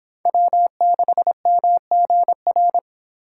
Morse code, heard as a keyed tone, about 25 wpm, 700 Hz. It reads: W6MGR